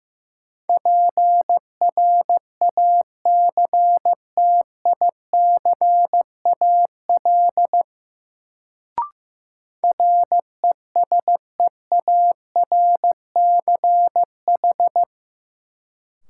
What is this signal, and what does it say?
Morse code, heard as a keyed tone, about 15 wpm, 700 Hz. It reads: PRACTICAL RESEARCH
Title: Sentences from Top 1400 Words - New Word Drill Alt - 15wpm